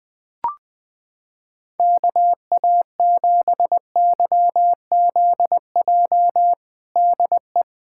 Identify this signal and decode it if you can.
Morse code, heard as a keyed tone, about 20 wpm, 700 Hz. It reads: KA7YZJ DE
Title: USA Callsign DE Callsign - 6 Characters - 20wpm